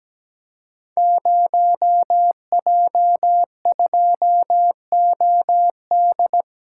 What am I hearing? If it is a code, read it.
Morse code, heard as a keyed tone, about 17 wpm, 700 Hz. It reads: 0J2OD
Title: Code Groups - 5 Letter:Numbers - 17wpm